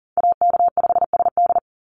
Morse code, heard as a keyed tone, about 40 wpm, 700 Hz. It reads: AK5SD
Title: NA SOTA Chaser Callsigns 40wpm